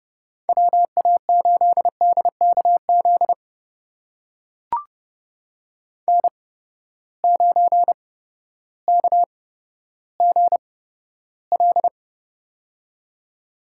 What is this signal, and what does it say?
Morse code, heard as a keyed tone, about 30 wpm, spaced as for 14 wpm, 700 Hz. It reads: WA8DKZ N9KGL
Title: USA Call Signs - 8x Character Spacing - 30wpm